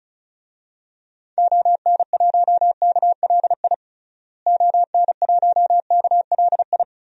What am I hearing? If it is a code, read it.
Morse code, heard as a keyed tone, about 36 wpm, 700 Hz. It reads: ON1KLI ON1KLI
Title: DXCC-100 Call Signs - 6 Characters - Repeated 3x - 35wpm